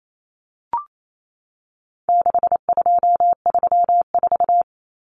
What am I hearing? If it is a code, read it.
Morse code, heard as a keyed tone, about 28 wpm, 700 Hz. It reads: 6234
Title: Common Cut Numbers - 4-Digit 28wpm